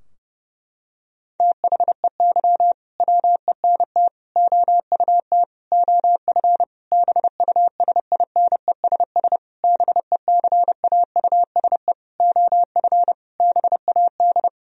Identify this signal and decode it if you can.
Morse code, heard as a keyed tone, about 30 wpm, 700 Hz. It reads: THEY WENT OUT OF BUSINESS BECAUSE OF BAD
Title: Sentences from Top 1000 Words - Review Part 4 - 30wpm